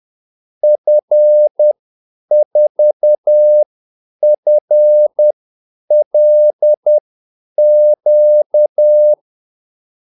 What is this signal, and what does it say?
Morse code, heard as a keyed tone, about 10 wpm, spaced as for 8 wpm, 600 Hz. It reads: F4FLQ
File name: Callsigns-10-8wpm